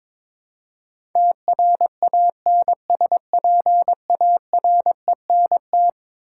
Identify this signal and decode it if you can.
Morse code, heard as a keyed tone, about 22 wpm, 700 Hz. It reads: TRANSPARENT
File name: practice / 22wpm / Eleven Letter Words - 22wpm